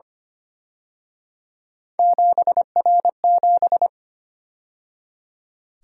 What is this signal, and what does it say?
Morse code, heard as a keyed tone, about 29 wpm, 700 Hz. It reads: 7R7
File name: Speed-Racing - Three Letter-Number Combinations - 25wpm